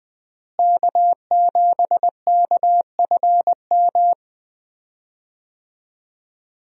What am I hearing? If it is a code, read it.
Morse code, heard as a keyed tone, about 21 wpm, 700 Hz. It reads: K7KFM
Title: USA Call Signs - 5 Characters - Repeated 2x - 20wpm